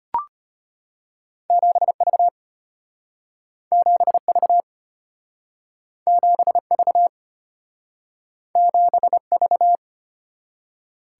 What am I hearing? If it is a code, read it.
Morse code, heard as a keyed tone, about 33 wpm, 700 Hz. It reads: 7V 7V 7V 7V